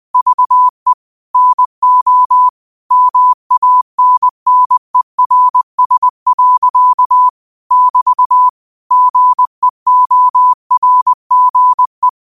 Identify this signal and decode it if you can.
Morse code, heard as a keyed tone, about 20 wpm, 1000 Hz. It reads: VE NO MANNERS. <BT> GEORGE